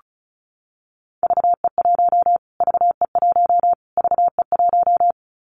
Morse code, heard as a keyed tone, about 34 wpm, 700 Hz. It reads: VE1 VE1 VE1
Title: Callsign Prefixes for Top 25 DXCC - Repeated 3x - 35wpm